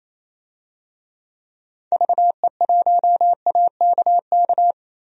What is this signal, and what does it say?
Morse code, heard as a keyed tone, about 27 wpm, 700 Hz. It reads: VE1AKK
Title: DXCC-100 Call Signs - 6 Characters - 28wpm